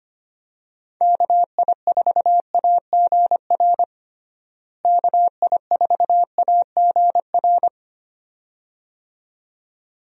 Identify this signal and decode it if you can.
Morse code, heard as a keyed tone, about 25 wpm, 700 Hz. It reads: KI4AGR KI4AGR
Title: USA Call Signs - 6 Characters - Repeated 2x - 25wpm